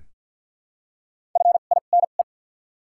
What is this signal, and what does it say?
Morse code, heard as a keyed tone, about 53 wpm, spaced as for 37 wpm, 700 Hz. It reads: FINE
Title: 2x Character Spacing - Top 500 Words - 50wpm